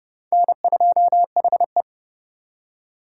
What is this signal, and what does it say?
Morse code, heard as a keyed tone, about 38 wpm, 700 Hz. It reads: N2HE
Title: Speed Racing - USA 1x2 Call Signs 30wpm